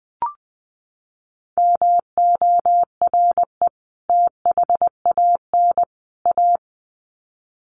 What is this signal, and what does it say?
Morse code, heard as a keyed tone, about 20 wpm, 700 Hz. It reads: MORE THAN A
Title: Sets of 3 Words 20wpm